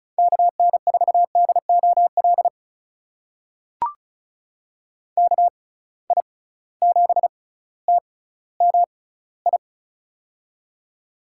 Morse code, heard as a keyed tone, about 34 wpm, spaced as for 18 wpm, 700 Hz. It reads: KN4DOL KI7TMI
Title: USA Call Signs - 6 Characters - 6x Character Spacing - 35wpm